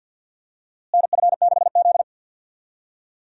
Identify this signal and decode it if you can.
Morse code, heard as a keyed tone, about 53 wpm, 700 Hz. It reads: NF6Z
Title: USA Call Signs - 4 Characters - Repeated 3x - 50wpm